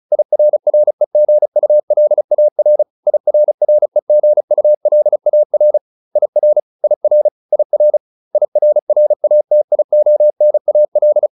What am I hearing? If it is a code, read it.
Morse code, heard as a keyed tone, about 34 wpm, 600 Hz. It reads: IRREGULAR IRREGULAR IR IR IR IRRATIONAL